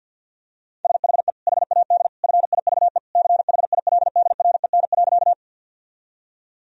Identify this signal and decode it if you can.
Morse code, heard as a keyed tone, about 50 wpm, 700 Hz. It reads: SHE HAD FIVE CHILDREN.
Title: Sentences from Top 1000 Words - Review Part 2 - 50wpm